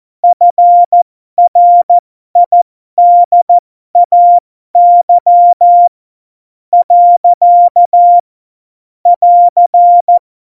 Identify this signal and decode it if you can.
Morse code, heard as a keyed tone, about 14 wpm, spaced as for 12 wpm, 700 Hz. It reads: FRIDAY . <AR>
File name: Morsekurs / Witze